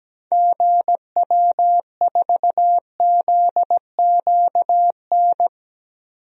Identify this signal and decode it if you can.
Morse code, heard as a keyed tone, about 17 wpm, 700 Hz. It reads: GW4ZQN
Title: DXCC-100 Call Signs - 6 Characters - Repeated 3x - 17wpm